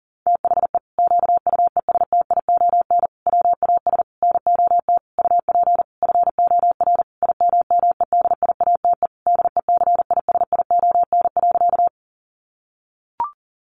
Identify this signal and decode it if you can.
Morse code, heard as a keyed tone, about 40 wpm, 700 Hz. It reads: THE QUESTION WAS NOT UP FOR IMMEDIATE DECISION.
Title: Sentences from Top 1200 Words 40wpm